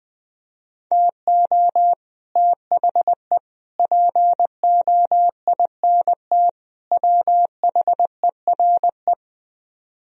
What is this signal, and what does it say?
Morse code, heard as a keyed tone, about 20 wpm, 700 Hz. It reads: TO THE POINT WHERE